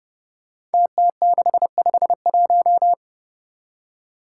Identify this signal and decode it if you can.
Morse code, heard as a keyed tone, about 30 wpm, 700 Hz. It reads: TT651
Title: Five Challenging Letter-Number Combinations 30wpm